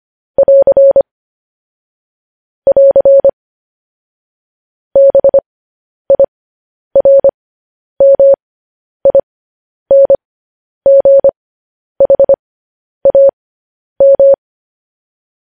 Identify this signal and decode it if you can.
Morse code, heard as a keyed tone, about 25 wpm, spaced as for 10 wpm, 550 Hz. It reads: <AR> <AR> BIRMINGHAM
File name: podcast / News headlines-1x-25x10